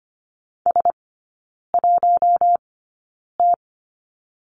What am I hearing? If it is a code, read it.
Morse code, heard as a keyed tone, about 25 wpm, spaced as for 9 wpm, 700 Hz. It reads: S1T
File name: course / 25:10wpm / 093 Lesson 16e - Letter c - Three Characters 25wpm